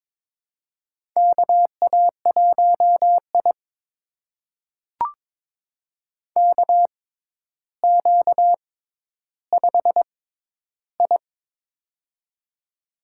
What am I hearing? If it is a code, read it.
Morse code, heard as a keyed tone, about 22 wpm, spaced as for 11 wpm, 700 Hz. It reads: KA1I KQ5I
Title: USA Call Signs - 4 Characters - 6x Character Spacing - 22wpm